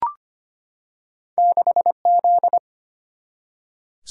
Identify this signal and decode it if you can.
Morse code, heard as a keyed tone, about 25 wpm, 700 Hz. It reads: 6 Z
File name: Two Letter-Number Combinations - 25wpm